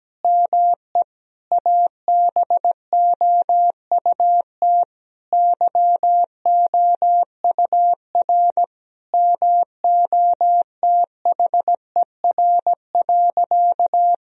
Morse code, heard as a keyed tone, about 17 wpm, 700 Hz. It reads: ME ABOUT YOUR MOTHER.
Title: Sentences from Top 1000 Words - Review Part 3 - 17wpm